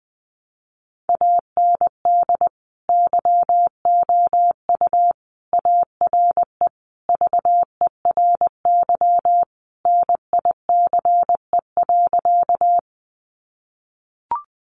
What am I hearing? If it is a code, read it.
Morse code, heard as a keyed tone, about 20 wpm, 700 Hz. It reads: AND YOU ARE VERY NICE.
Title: Sentences from Top 1100 Words 20wpm